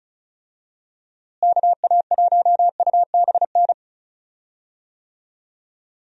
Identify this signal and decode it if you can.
Morse code, heard as a keyed tone, about 34 wpm, 700 Hz. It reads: KA1UBN
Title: USA Call Signs - 6 Characters - Repeated 2x - 35wpm